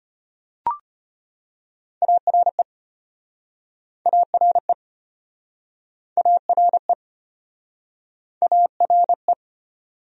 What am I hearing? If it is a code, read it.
Morse code, heard as a keyed tone, about 33 wpm, 700 Hz. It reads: ARE ARE ARE ARE